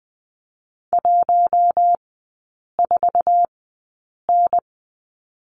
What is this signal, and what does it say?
Morse code, heard as a keyed tone, about 20 wpm, spaced as for 8 wpm, 700 Hz. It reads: 14N
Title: 060 Lesson 11e - Letter h - Three Characters 20wpm